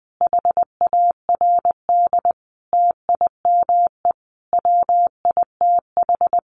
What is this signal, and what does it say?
Morse code, heard as a keyed tone, about 20 wpm, 700 Hz. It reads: HARD TIME WITH
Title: Sentences from Top 300 Words 20wpm